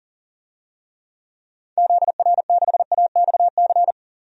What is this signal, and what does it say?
Morse code, heard as a keyed tone, about 40 wpm, 700 Hz. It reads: ZR6AXC
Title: DXCC-100 Call Signs - 6 Characters - 40wpm